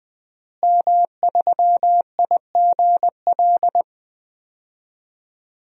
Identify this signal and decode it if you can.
Morse code, heard as a keyed tone, about 20 wpm, 700 Hz. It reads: M3IGL
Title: DXCC-100 Call Signs - 5 Characters - Repeated 2x - 20wpm